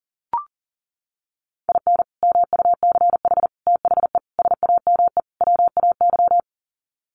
Morse code, heard as a keyed tone, about 40 wpm, 700 Hz. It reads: IN MUCH THE SAME WAY
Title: Sets of 5 Words Encore 40wpm